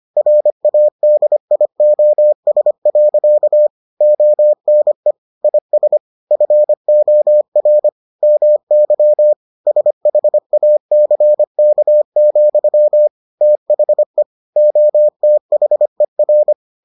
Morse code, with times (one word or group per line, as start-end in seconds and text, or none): 0.2-3.7 RADIOS.
4.0-5.1 ONE
5.4-6.0 IS
6.3-7.9 FOR
8.2-9.3 MY
9.7-13.1 SHACK,
13.4-14.2 THE
14.6-16.5 OTHER